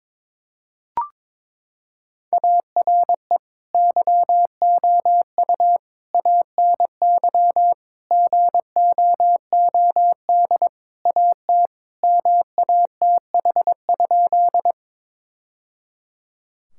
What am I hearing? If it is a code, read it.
Morse code, heard as a keyed tone, about 22 wpm, 700 Hz. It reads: ARE YOU ANY GOOD AT MATH?
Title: Sentences from Top 1000 Words - Review Part 6 - 22wpm